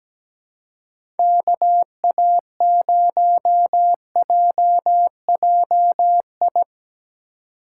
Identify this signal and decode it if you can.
Morse code, heard as a keyed tone, about 20 wpm, 700 Hz. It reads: KA0JJI
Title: Speed Racing - USA Call Signs 17wpm